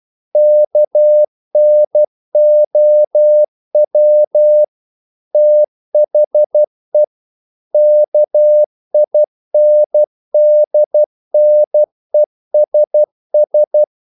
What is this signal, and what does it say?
Morse code, heard as a keyed tone, about 12 wpm, 600 Hz. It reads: KNOW THE KINDNESS
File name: mp3 files / the gardener - 12wpm